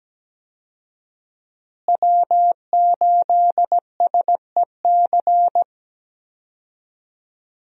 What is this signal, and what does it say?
Morse code, heard as a keyed tone, about 17 wpm, 700 Hz. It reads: W8SEC
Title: USA Call Signs - 5 Characters - Repeated 2x - 17wpm